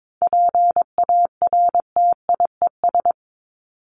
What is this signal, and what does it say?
Morse code, heard as a keyed tone, about 22 wpm, 700 Hz. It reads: PARTIES
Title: ES:S Suffix - Top 100 Words with Base Words 22wpm